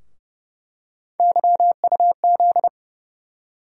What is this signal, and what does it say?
Morse code, heard as a keyed tone, about 31 wpm, 700 Hz. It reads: YUZ
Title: USA Call Sign Postfixes - 3 Letters 30wpm